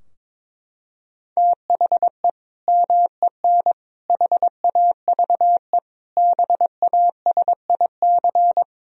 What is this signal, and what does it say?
Morse code, heard as a keyed tone, about 21 wpm, 700 Hz. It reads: THE MEN HAVE BASIC
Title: Sentences from Top 1000 Words - Review Part 3 - 22wpm